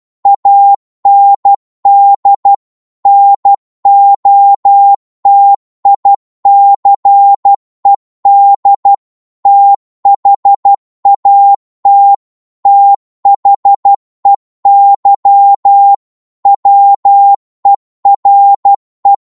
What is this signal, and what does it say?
Morse code, heard as a keyed tone, about 12 wpm, 800 Hz. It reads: AND NOTICED THAT THEY WERE
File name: cwmp3 / psg-1-12wpm